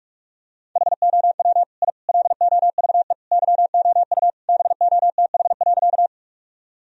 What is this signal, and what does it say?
Morse code, heard as a keyed tone, about 46 wpm, 700 Hz. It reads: HOW I LOVE YOU BOTH.
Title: Sentences from Top 1000 Words - Review Part 1 - 45wpm